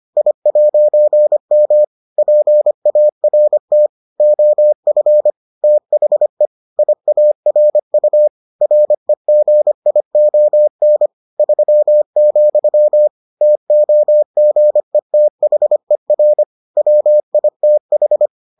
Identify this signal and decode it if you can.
Morse code, heard as a keyed tone, about 25 wpm, 600 Hz. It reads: I'M PART OF THE IARU REGION 3, TOGETHER WITH